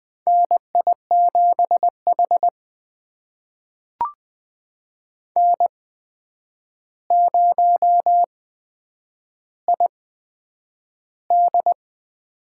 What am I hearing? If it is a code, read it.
Morse code, heard as a keyed tone, about 20 wpm, spaced as for 9 wpm, 700 Hz. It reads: NI7H N0ID
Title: USA Call Signs - 4 Characters - 8x Character Spacing - 20wpm